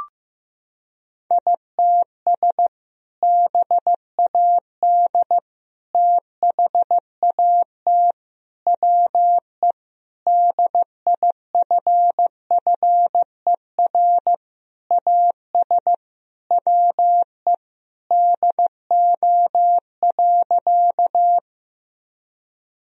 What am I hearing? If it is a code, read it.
Morse code, heard as a keyed tone, about 16 wpm, 700 Hz. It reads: ITS BAD THAT WE DIFFER AS WE DO.
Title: Sentences from Top 1000 Words - Review Part 1 - 15wpm